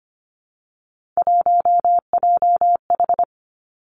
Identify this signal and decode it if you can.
Morse code, heard as a keyed tone, about 25 wpm, 700 Hz. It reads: 1JH
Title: Three Challenging Letter-Number Combinations 25wpm